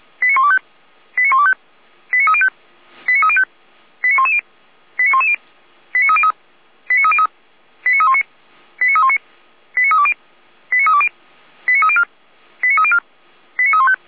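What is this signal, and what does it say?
Five-tone selective call (ZVEI), heard as quick runs of five tones, tones of about 70 ms. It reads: 89125 89125 89305 89305 89110 89110 89303 89303 89319 89319 89320 89320 89304 89304 89315
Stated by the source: ZVEI five-tone selective call received on an SDR, from the Feuerwehr Orlamünde fire brigade.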